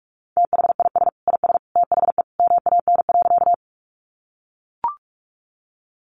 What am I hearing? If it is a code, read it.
Morse code, heard as a keyed tone, about 44 wpm, 700 Hz. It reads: THIS IS THE MAN.
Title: Sentences from Top 200 Words 45wpm